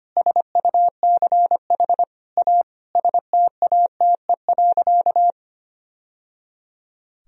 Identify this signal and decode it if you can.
Morse code, heard as a keyed tone, about 25 wpm, 700 Hz. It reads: SUCH A STATE.
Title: Sentences from Top 1000 Words - Review Part 2 - 25wpm